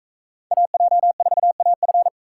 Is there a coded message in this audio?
Yes: Morse code from 0.5 to 2.1 s, heard as a keyed tone, about 42 wpm, 700 Hz. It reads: AJ4AF